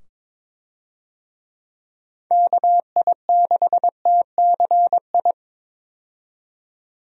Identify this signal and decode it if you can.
Morse code, heard as a keyed tone, about 22 wpm, 700 Hz. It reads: KI6TCI
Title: USA Call Signs - 6 Characters - 22wpm